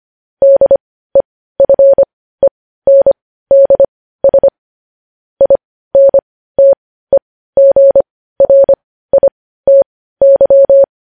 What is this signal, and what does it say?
Morse code, heard as a keyed tone, about 25 wpm, spaced as for 15 wpm, 550 Hz. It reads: DEFENDS INTEGRITY
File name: podcast / News headlines-1x-25x15